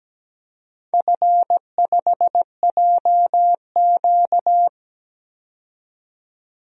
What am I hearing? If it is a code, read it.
Morse code, heard as a keyed tone, about 17 wpm, 700 Hz. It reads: F5JQ
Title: DXCC-100 Call Signs - 4 Characters - 17wpm